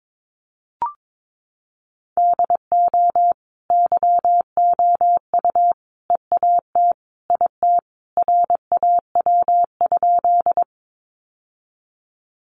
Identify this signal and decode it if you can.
Morse code, heard as a keyed tone, about 22 wpm, 700 Hz. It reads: DO YOU EAT IT RAW?
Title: Sentences from Top 1500 Words 22wpm